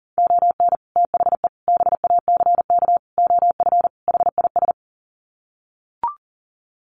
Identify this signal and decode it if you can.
Morse code, heard as a keyed tone, about 37 wpm, 700 Hz. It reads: ON THE BACK OF HIS
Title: Sets of 5 Words Encore 40wpm